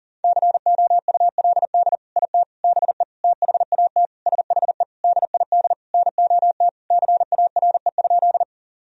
Morse code, heard as a keyed tone, about 40 wpm, 700 Hz. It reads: COULD IT BE THAT SHE DID NOT CARE?
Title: Sentences from Top 1000 Words - Review Part 1 - 40wpm